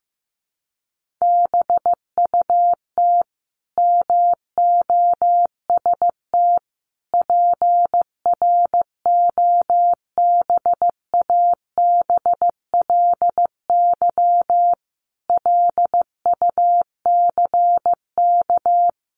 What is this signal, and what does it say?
Morse code, heard as a keyed tone, about 15 wpm, 700 Hz. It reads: BUT MOST PROBABLY LUCK
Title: Sentences from Top 1500 Words 15wpm